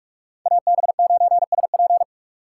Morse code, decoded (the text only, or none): AB9SP